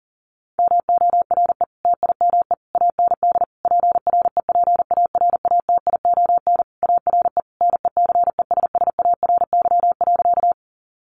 Morse code, heard as a keyed tone, about 40 wpm, 700 Hz. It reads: MORE TIME AND PREPARATION ARE NECESSARY.
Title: Sentences from Top 1400 Words 40wpm